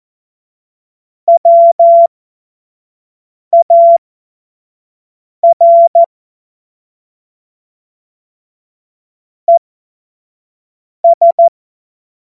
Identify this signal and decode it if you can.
Morse code, heard as a keyed tone, about 14 wpm, spaced as for 5 wpm, 680 Hz. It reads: WAR ES
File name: Morsekurs / Witze / Witz 27. Dezember 2022 25 F70